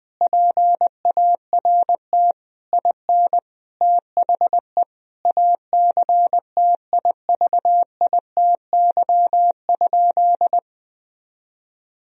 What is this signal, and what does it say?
Morse code, heard as a keyed tone, about 20 wpm, 700 Hz. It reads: PART IN THE ACTIVITY?
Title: Sentences from Top 1000 Words - Review Part 4 - 20wpm